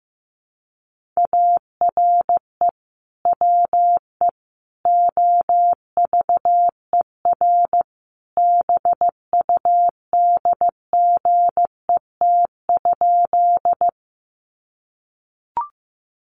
Morse code, heard as a keyed tone, about 15 wpm, 700 Hz. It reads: ARE WE OVER BUDGET?
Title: Sentences from Top 1200 Words 15wpm